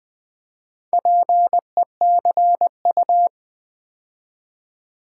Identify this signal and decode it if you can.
Morse code, heard as a keyed tone, about 20 wpm, 700 Hz. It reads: PECU
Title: Build A Word - 2 Letters At A Time - 8-Letter Words - 20wpm